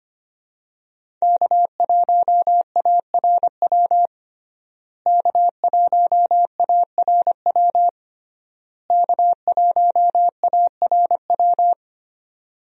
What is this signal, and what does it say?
Morse code, heard as a keyed tone, about 25 wpm, 700 Hz. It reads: K1ARW K1ARW K1ARW
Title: USA Call Signs - 5 Characters - Repeated 3x - 25wpm